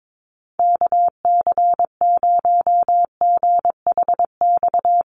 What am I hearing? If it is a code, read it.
Morse code, heard as a keyed tone, about 22 wpm, 700 Hz. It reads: KC0GHX
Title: USA Call Signs Repeated 3x 22wpm